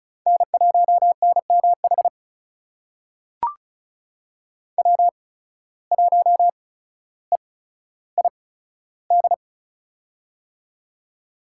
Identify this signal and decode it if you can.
Morse code, heard as a keyed tone, about 34 wpm, spaced as for 15 wpm, 700 Hz. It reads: N1NMH W1EID
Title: USA Call Signs - 5 Characters - 8x Character Spacing - 35wpm